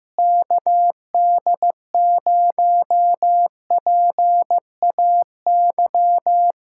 Morse code, heard as a keyed tone, about 15 wpm, 700 Hz. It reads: KD0PAY
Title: Speed-Racing Common QSO Phrases Encore - 15wpm